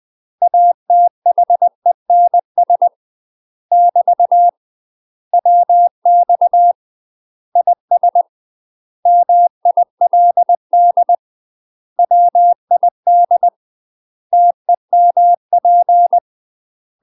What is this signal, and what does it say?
Morse code, heard as a keyed tone, about 20 wpm, 700 Hz. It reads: ATHENS <BT> WX IS MILD WID TEMP
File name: UK MASTER QSOs TRAINING / 20WPM / UK MASTER QSOs 0068